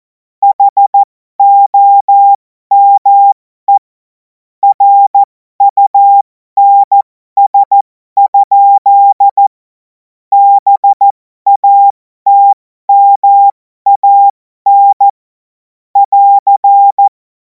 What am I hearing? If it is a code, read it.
Morse code, heard as a keyed tone, about 14 wpm, spaced as for 12 wpm, 800 Hz. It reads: HOME RUNS? BATMAN <AR>
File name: Morsekurs / Witze